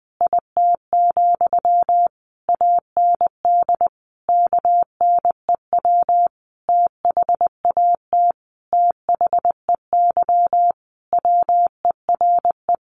Morse code, heard as a keyed tone, about 20 wpm, 700 Hz. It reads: IT, AND KNEW THAT THEY WERE
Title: Sentences from Top 1200 Words 20wpm